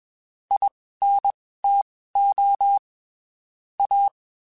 Morse code, heard as a keyed tone, about 21 wpm, spaced as for 15 wpm, 800 Hz. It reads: INTO A
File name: (spaced) / QSOs Spaced-mp3s / kc7sac @20 sp